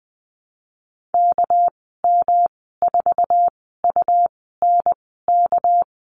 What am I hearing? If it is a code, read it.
Morse code, heard as a keyed tone, about 20 wpm, spaced as for 15 wpm, 700 Hz. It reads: KM4UNK
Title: 231 Lesson 37g - Letter x - Call Signs 20wpm